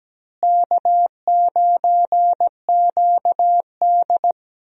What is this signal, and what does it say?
Morse code, heard as a keyed tone, about 17 wpm, 700 Hz. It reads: K9QD
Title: USA Call Signs - 4 Characters - Repeated 2x - 17wpm